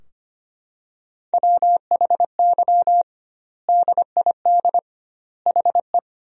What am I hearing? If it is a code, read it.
Morse code, heard as a keyed tone, about 25 wpm, 700 Hz. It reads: WHY DID HE
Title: 2x Word-Spacing - Sentences from Top 200 Words - 25wpm